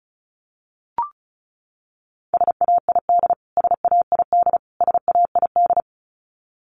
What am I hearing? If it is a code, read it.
Morse code, heard as a keyed tone, about 34 wpm, 700 Hz. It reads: SAID SAID SAID
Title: Top 200 Words Repeated 35wpm